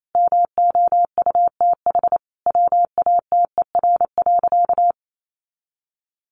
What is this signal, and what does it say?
Morse code, heard as a keyed tone, about 28 wpm, 700 Hz. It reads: MOUTH WATER.
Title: Sentences from Top 1500 Words 28wpm